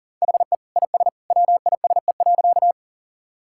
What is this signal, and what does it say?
Morse code, heard as a keyed tone, about 40 wpm, 700 Hz. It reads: HE IS WISE.
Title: Sentences from Top 1600 Words 40wpm